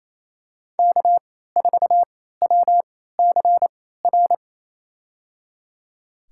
Morse code, heard as a keyed tone, about 28 wpm, spaced as for 16 wpm, 700 Hz. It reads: K4WCR
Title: USA Call Signs - 3x Character Spacing - 28wpm